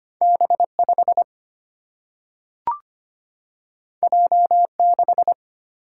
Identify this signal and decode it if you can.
Morse code, heard as a keyed tone, about 25 wpm, 700 Hz. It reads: B5 J6
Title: Single-Letter Single-Number 25wpm